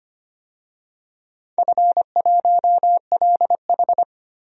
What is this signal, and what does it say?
Morse code, heard as a keyed tone, about 25 wpm, 700 Hz. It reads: F1LH
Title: DXCC-100 Call Signs - 4 Characters - 25wpm